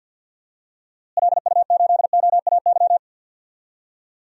Morse code, heard as a keyed tone, about 49 wpm, 700 Hz. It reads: LU8OAY
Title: DXCC-100 Call Signs - 6 Characters - Repeated 3x - 50wpm